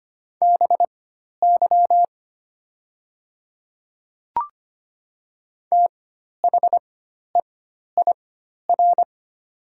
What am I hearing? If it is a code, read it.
Morse code, heard as a keyed tone, about 25 wpm, spaced as for 12 wpm, 700 Hz. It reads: BY THEIR
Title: 4x Character Spacing - Top 100 Words - 25wpm